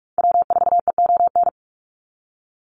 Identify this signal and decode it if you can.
Morse code, heard as a keyed tone, about 44 wpm, 700 Hz. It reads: W4EON
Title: NA SOTA Activator Callsigns 45wpm